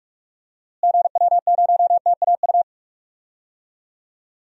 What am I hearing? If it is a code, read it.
Morse code, heard as a keyed tone, about 44 wpm, 700 Hz. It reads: GW0TAU
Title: DXCC-100 Call Signs - 6 Characters - Repeated 3x - 45wpm